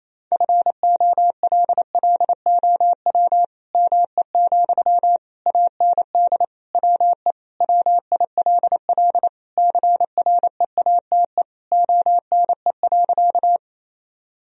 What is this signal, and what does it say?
Morse code, heard as a keyed tone, about 28 wpm, 700 Hz. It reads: FOLLOW ME, AND WE WILL CREATE ONE.
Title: Sentences from Top 1000 Words - Review Part 3 - 28wpm